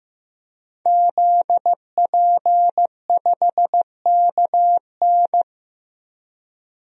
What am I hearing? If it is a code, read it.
Morse code, heard as a keyed tone, about 15 wpm, 700 Hz. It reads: ZP5KN
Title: DXCC-100 Call Signs - 5 Characters - Repeated 2x - 15wpm